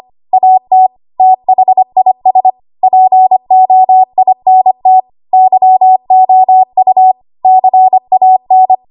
Morse code, heard as a keyed tone, about 25 wpm, 750 Hz. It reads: AT THIS POINT YOU CAN
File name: Archive / 25 WPM / 25WPM